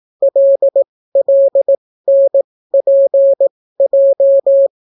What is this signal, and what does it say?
Morse code, heard as a keyed tone, about 18 wpm, spaced as for 14 wpm, 550 Hz. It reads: LLNPJ